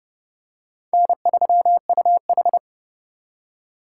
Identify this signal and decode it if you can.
Morse code, heard as a keyed tone, about 28 wpm, 700 Hz. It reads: N3UH